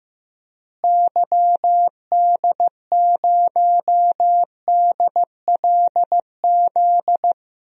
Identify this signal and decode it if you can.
Morse code, heard as a keyed tone, about 15 wpm, 700 Hz. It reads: YD0DLZ
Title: DXCC-100 Call Signs - 6 Characters - Repeated 3x - 15wpm